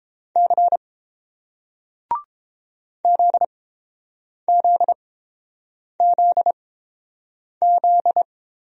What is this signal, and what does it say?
Morse code, heard as a keyed tone, about 30 wpm, spaced as for 7 wpm, 700 Hz. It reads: C ZZZZ